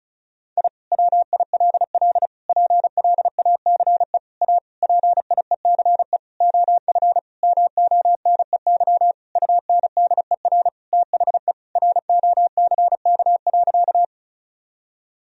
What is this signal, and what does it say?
Morse code, heard as a keyed tone, about 34 wpm, 700 Hz. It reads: I WILL PLACE A PIECE OF MONEY UNDER THE ROCK.
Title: Sentences from Top 1000 Words - Review Part 2 - 35wpm